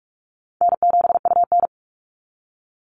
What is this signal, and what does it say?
Morse code, heard as a keyed tone, about 44 wpm, 700 Hz. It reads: N7UN
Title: NA SOTA Chaser Callsigns 45wpm